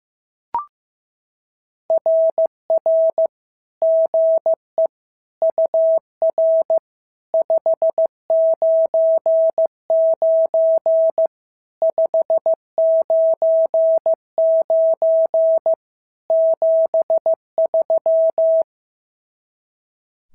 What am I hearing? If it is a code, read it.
Morse code, heard as a keyed tone, about 14 wpm, 650 Hz. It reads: RR GE UR 599 599 73
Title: SOTA QSO Practice 15wpm